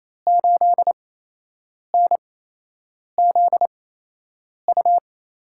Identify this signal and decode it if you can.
Morse code, heard as a keyed tone, about 27 wpm, spaced as for 8 wpm, 700 Hz. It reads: 8NZU
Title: USA Call Signs - 6 Characters - 8x Character Spacing - 28wpm